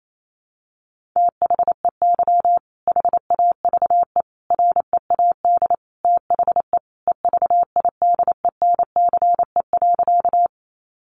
Morse code, heard as a keyed tone, about 28 wpm, 700 Hz. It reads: THEY HAVE READ THE EVIDENCE.
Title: Sentences from Top 1300 Words 28wpm